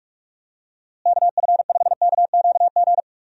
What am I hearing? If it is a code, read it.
Morse code, heard as a keyed tone, about 41 wpm, 700 Hz. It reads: KF5KQG
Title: USA Call Signs - 6 Characters - Repeated 2x - 45wpm